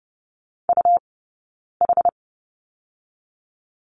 Morse code, heard as a keyed tone, about 30 wpm, spaced as for 9 wpm, 690 Hz. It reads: UH
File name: Lesson 18d - Letter m - Two Characters 30wpm